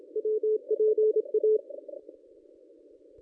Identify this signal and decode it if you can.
Morse code, heard as a keyed tone, about 28 wpm, 440 Hz. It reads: WPA
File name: contests / ss-cw-k5tr-op-n5rz -1100